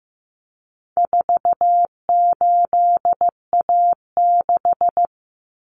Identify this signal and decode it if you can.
Morse code, heard as a keyed tone, about 15 wpm, 700 Hz. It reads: 48A6
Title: Common Cut Numbers - 4-Digit 15wpm